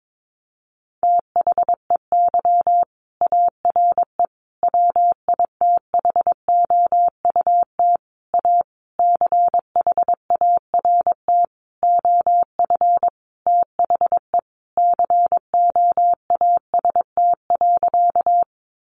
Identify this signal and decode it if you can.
Morse code, heard as a keyed tone, about 22 wpm, 700 Hz. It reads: THEY ARE WITHOUT A CHART OF THE COAST.
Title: Sentences from Top 1300 Words 22wpm